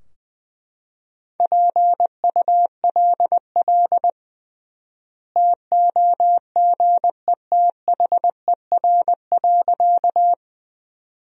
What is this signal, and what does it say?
Morse code, heard as a keyed tone, about 20 wpm, 700 Hz. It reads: PULL TOGETHER.
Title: 3x Word-Spacing - Sentences from Top 400 Words - 20wpm